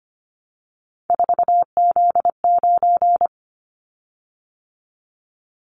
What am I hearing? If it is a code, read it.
Morse code, heard as a keyed tone, about 25 wpm, 700 Hz. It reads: 4Z9
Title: Callsign Prefixes for Top 50 DXCC - 25wpm